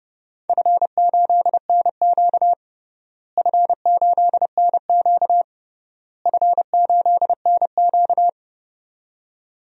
Morse code, heard as a keyed tone, about 32 wpm, 700 Hz. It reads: F8NQ F8NQ F8NQ
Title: DXCC-100 Call Signs - 4 Characters - Repeated 3x - 30wpm